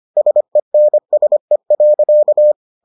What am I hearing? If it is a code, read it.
Morse code, heard as a keyed tone, about 25 wpm, 600 Hz. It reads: SENSE.